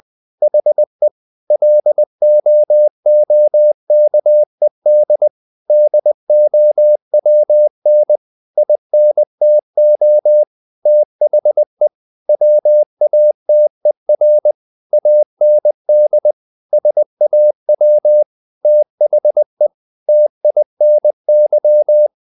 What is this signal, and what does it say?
Morse code, heard as a keyed tone, about 20 wpm, 600 Hz. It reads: HE LOOKED DOWN INTO THE WATER AND SAW THE TINY
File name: the old man and the sea / the old man and the sea part 1 20wpm